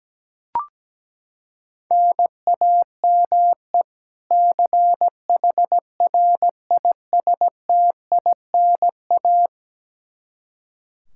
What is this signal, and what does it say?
Morse code, heard as a keyed tone, about 17 wpm, 700 Hz. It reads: NAME CHRISTINA
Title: Practice QSO - Exchange 1 - Beginning - 17wpm